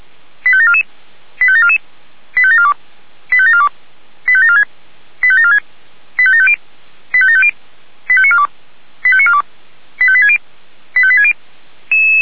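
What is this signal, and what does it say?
Five-tone selective call (ZVEI), heard as quick runs of five tones, tones of about 70 ms. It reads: 85744 85744 85742 85742 85746 85746 85750 85750 85942 85942 85860 85860
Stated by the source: ZVEI five-tone selective call received on an SDR, fire brigade feuerwehr-orlamuende.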